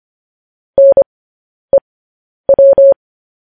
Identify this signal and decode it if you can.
Morse code, heard as a keyed tone, about 25 wpm, spaced as for 10 wpm, 550 Hz. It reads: NEW
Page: news headlines in morse code at 10 wpm